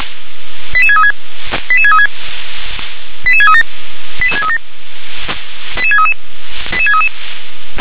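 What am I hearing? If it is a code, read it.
Five-tone selective call (ZVEI), heard as quick runs of five tones, tones of about 70 ms. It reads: 88636 88636 88637 88637 88633 88633
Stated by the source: ZVEI five-tone selective call received by SDR.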